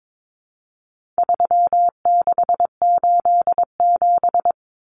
Morse code, heard as a keyed tone, about 22 wpm, 700 Hz. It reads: 3687
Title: Numbers - 4-Digit 22wpm